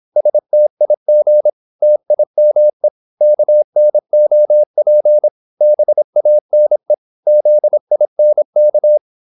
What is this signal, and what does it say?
Morse code, heard as a keyed tone, about 26 wpm, 600 Hz. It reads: STIG TIME KNOP BANE ZINK
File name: Korte ord DK 26wpm